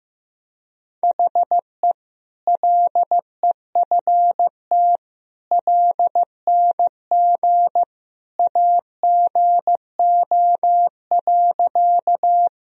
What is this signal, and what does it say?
Morse code, heard as a keyed tone, about 15 wpm, 700 Hz. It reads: HE LEFT LNG AGO.
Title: Sentences from Top 500 Words and Abbreviations 15wpm